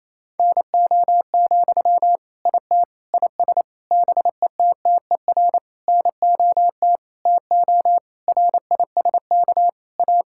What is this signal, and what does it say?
Morse code, heard as a keyed tone, about 28 wpm, 700 Hz. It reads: NO, IT IS BETTER NOT TO RISK A
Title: Sentences from Top 1600 Words 28wpm